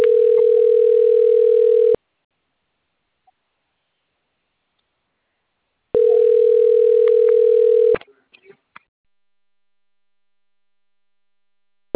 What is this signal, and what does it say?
Ringback tone (North America), 440 + 480 Hz, with speech.
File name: rg-1600-unknown- - - .6465